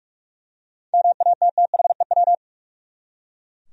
Morse code, heard as a keyed tone, about 44 wpm, 700 Hz. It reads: MATTHEW
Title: USA Common Names - 45wpm